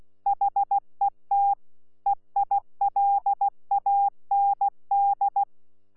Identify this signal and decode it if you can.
Morse code, heard as a keyed tone, about 16 wpm, 800 Hz. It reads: HET EILAND
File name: Z PI4VRZ / PI4VRZ 14-jun-2025